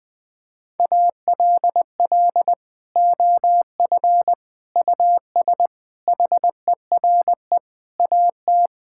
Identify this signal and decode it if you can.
Morse code, heard as a keyed tone, about 20 wpm, 700 Hz. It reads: ALL OF US HERE AT
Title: Sentences with 5-Word Phrases - 20wpm